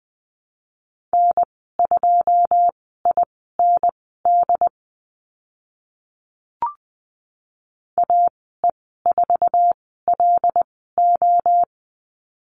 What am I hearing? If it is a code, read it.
Morse code, heard as a keyed tone, about 20 wpm, spaced as for 15 wpm, 700 Hz. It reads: N2IND AE4LO
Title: 082 Lesson 14g - Number 2 - Call Signs 20wpm